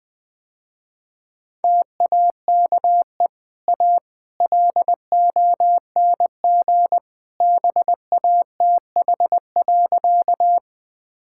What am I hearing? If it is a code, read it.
Morse code, heard as a keyed tone, about 20 wpm, 700 Hz. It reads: TAKE A LONG BATH.
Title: Easy Sentences with Hints - 20wpm